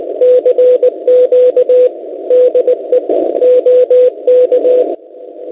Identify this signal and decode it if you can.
Morse code, heard as a keyed tone, about 18 wpm, 500 Hz, with radio noise. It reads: CQ DE OK
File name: Pixid2 / Zvuk / sdr.vebik.cz 3520.00 cw